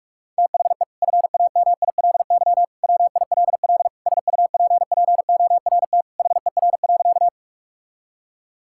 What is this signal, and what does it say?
Morse code, heard as a keyed tone, about 46 wpm, 700 Hz. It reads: THE FAMILY WILL SUPPORT HER.
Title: Sentences from Top 1000 Words - Review Part 5 - 45wpm